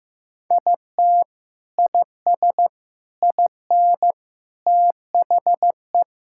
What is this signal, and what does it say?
Morse code, heard as a keyed tone, about 16 wpm, 700 Hz. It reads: IT IS IN THE
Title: Sentences from Top 1000 Words - Review Part 5 - 15wpm